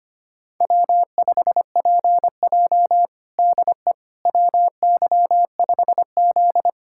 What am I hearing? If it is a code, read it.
Morse code, heard as a keyed tone, about 25 wpm, 700 Hz. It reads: W5PJ DE WY5Z
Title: USA Callsign DE Callsign - 4 Characters - 25wpm